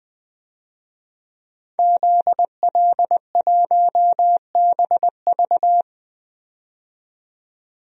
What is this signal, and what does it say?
Morse code, heard as a keyed tone, about 21 wpm, 700 Hz. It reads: ZL1BV
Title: DXCC-100 Call Signs - 5 Characters - 20wpm